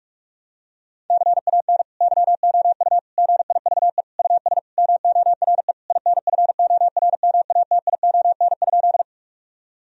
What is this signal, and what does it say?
Morse code, heard as a keyed tone, about 46 wpm, 700 Hz. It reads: CAN YOU GIVE US MORE INFORMATION?
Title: Sentences from Top 1000 Words - Review Part 3 - 45wpm